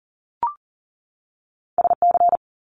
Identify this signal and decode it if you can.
Morse code, heard as a keyed tone, about 40 wpm, 700 Hz. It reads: SC
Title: ARRL Sections 40wpm